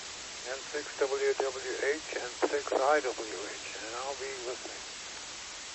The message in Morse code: none

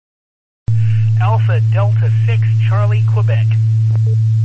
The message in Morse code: none

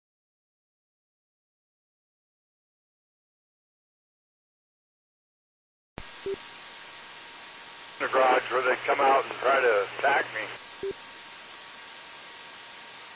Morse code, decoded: none

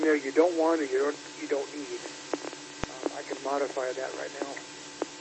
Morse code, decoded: none

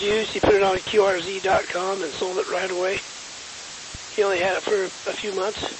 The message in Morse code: none

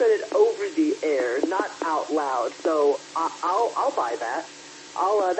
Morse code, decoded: none

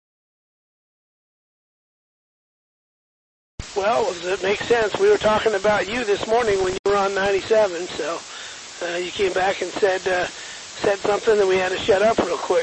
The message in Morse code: none

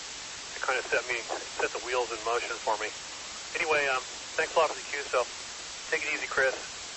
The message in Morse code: none

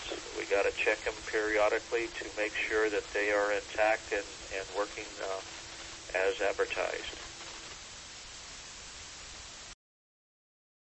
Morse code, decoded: none

